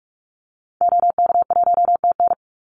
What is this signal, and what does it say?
Morse code, heard as a keyed tone, about 44 wpm, 700 Hz. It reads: OK1TN
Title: Call Signs Repeated 3x - 45wpm